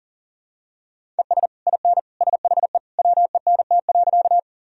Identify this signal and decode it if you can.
Morse code, heard as a keyed tone, about 40 wpm, 700 Hz. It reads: ES IN SHE WENT.
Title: Sentences from Top 200 Words and Abbreviations 40wpm